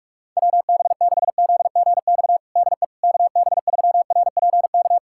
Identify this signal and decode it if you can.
Morse code, heard as a keyed tone, about 46 wpm, 700 Hz. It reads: WB6ZGX DE KB3RPK
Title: USA Callsign DE Callsign 45wpm